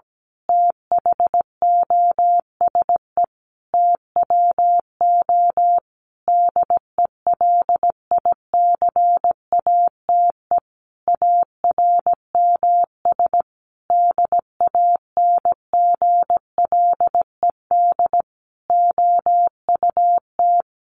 Morse code, heard as a keyed tone, about 17 wpm, 700 Hz. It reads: THOSE TWO DELICATE ARMS DANGLED OUT
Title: A Sound Of Thunder 17wpm